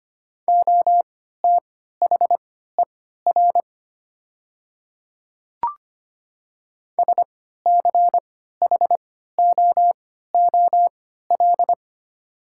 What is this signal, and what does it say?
Morse code, heard as a keyed tone, about 25 wpm, spaced as for 14 wpm, 700 Hz. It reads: OTHER SCHOOL
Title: 3x Character Spacing - Top 300 Words - 25wpm